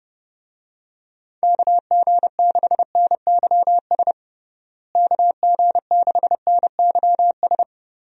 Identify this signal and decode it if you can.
Morse code, heard as a keyed tone, about 30 wpm, 700 Hz. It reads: KG6NYS KG6NYS
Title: USA Call Signs - 6 Characters - Repeated 3x - 30wpm